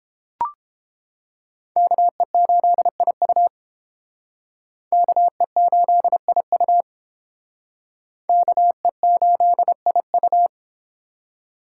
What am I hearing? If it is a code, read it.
Morse code, heard as a keyed tone, about 29 wpm, 700 Hz. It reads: KE8IU KE8IU KE8IU